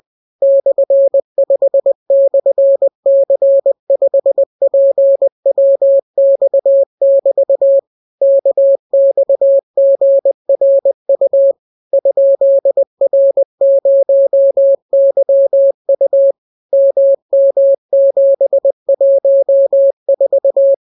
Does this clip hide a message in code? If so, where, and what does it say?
Yes: Morse code from 0.4 to 20.7 s, heard as a keyed tone, about 20 wpm, 550 Hz. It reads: /5/C5PWX<BT> KXGRU ?R0YU MM714